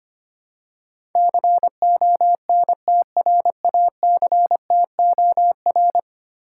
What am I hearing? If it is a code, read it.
Morse code, heard as a keyed tone, about 25 wpm, 700 Hz. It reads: CONTRACTOR
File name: 10 Letter Words - First 3 Letters Given - 25wpm